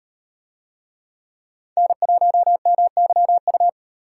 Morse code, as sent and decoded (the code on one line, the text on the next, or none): -. .---- -- -.-- ..-
N1MYU